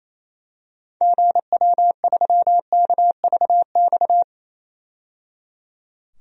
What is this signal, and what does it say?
Morse code, heard as a keyed tone, about 28 wpm, 700 Hz. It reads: GW3KVX
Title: DXCC-100 Call Signs - 6 Characters - 28wpm